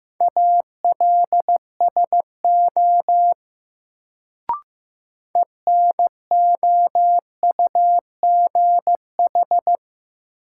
Morse code, heard as a keyed tone, about 15 wpm, 700 Hz. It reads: ALSO ENOUGH
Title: Sentences from Top 400 Words - New Word Drill - 15wpm